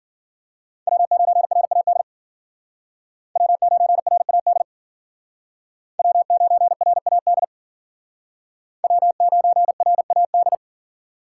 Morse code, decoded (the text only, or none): W9RAD W9RAD W9RAD W9RAD